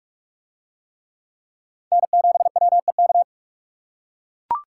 Morse code, heard as a keyed tone, about 44 wpm, 700 Hz. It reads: N7WEK